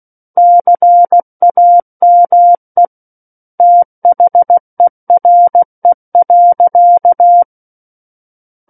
Morse code, heard as a keyed tone, about 16 wpm, 700 Hz. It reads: CAME THERE.
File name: Book One / 07-Chapter 7 - 16 WPM